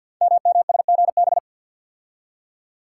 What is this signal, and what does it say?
Morse code, heard as a keyed tone, about 47 wpm, 700 Hz. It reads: MMSGB
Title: Code Groups - 5 Letters - 50wpm